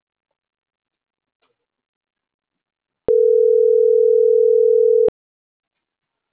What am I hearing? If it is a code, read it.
Ringback tone (North America), 440 + 480 Hz.